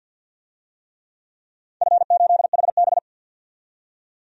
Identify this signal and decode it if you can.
Morse code, heard as a keyed tone, about 53 wpm, 700 Hz. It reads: F8HB
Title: DXCC-100 Call Signs - 4 Characters - 50wpm